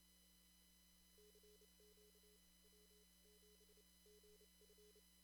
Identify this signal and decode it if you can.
Morse code, heard as a keyed tone, about 28 wpm, spaced as for 24 wpm, 420 Hz, with noise. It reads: CQ W7GF
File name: CW / kt5j wpx cw-op-k5pi